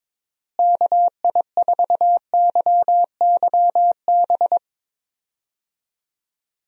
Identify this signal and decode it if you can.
Morse code, heard as a keyed tone, about 21 wpm, 700 Hz. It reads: KI4YYB
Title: USA Call Signs - 6 Characters - Repeated 3x - 22wpm